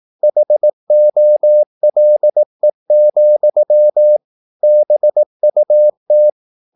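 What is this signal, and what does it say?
Morse code, heard as a keyed tone, about 18 wpm, 600 Hz. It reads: HOLE, BUT
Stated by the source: Morse practice at 18 wpm, English text, from CW med Gnister.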